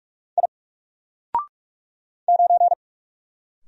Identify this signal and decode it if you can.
Morse code, heard as a keyed tone, about 43 wpm, 700 Hz. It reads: I 9
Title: Single Letter-Number Alt 45wpm